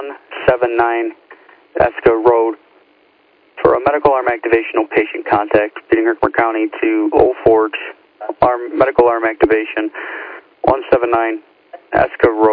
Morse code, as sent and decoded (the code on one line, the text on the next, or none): none